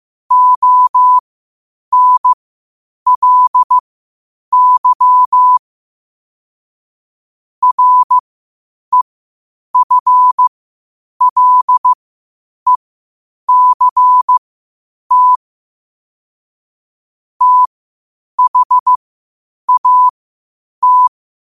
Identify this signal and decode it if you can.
Morse code, heard as a keyed tone, about 15 wpm, spaced as for 8 wpm, 1000 Hz. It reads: ONLY REFLECT THAT
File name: qotdfiles / 8wpm